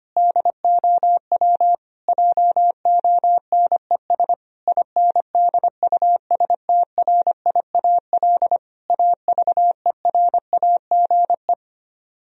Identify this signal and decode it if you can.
Morse code, heard as a keyed tone, about 24 wpm, 700 Hz. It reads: DOW JONES INDUSTRIAL AVERAGE
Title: Speed-Racing - Sets of 5 Words 25wpm